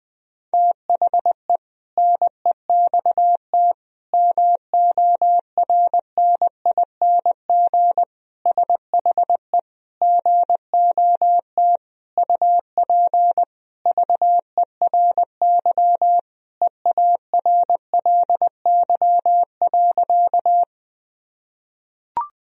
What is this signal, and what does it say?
Morse code, heard as a keyed tone, about 21 wpm, 700 Hz. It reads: THE NEXT MORNING SHE GOT UP VERY EARLY.
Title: Sentences from Top 1000 Words - Review Part 2 - 20wpm